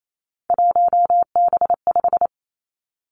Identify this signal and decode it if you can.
Morse code, heard as a keyed tone, about 28 wpm, 690 Hz. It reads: 1B5
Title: Three Challenging Letter-Number Combinations 28wpm